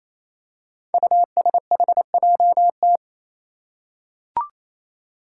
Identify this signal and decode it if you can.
Morse code, heard as a keyed tone, about 28 wpm, 700 Hz. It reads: USHJT
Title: Five Challenging Letter-Number Combinations 28wpm